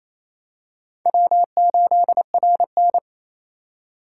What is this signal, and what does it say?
Morse code, heard as a keyed tone, about 28 wpm, 700 Hz. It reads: W8RN
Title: USA Call Signs - 4 Characters - 4x Character Spacing - 28wpm